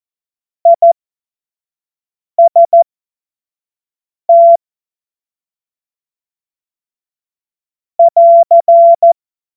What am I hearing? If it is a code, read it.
Morse code, heard as a keyed tone, about 14 wpm, spaced as for 5 wpm, 680 Hz. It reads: IST <AR>
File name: Morsekurs / Witze / Witz 16. November 2021, 25 F70